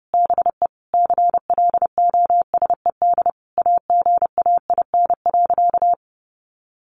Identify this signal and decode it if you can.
Morse code, heard as a keyed tone, about 30 wpm, 700 Hz. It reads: BE CLOSED AGAIN.
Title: Sentences from Top 1400 Words 30wpm